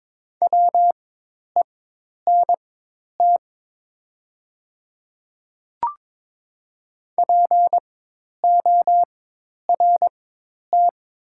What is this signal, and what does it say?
Morse code, heard as a keyed tone, about 22 wpm, spaced as for 10 wpm, 700 Hz. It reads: WENT PORT
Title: 4x Character Spacing - Top 200 Words - 22wpm